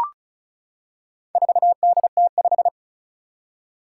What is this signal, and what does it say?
Morse code, heard as a keyed tone, about 36 wpm, 700 Hz. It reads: 4DT5
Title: Four Challenging Letter-Number Combinations 35wpm